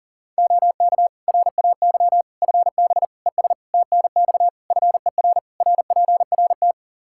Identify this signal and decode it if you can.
Morse code, heard as a keyed tone, about 37 wpm, 700 Hz. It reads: OK RAY FB ES TNX FER RPRT
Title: Practice QSO - Exchange 2 - Info - 40wpm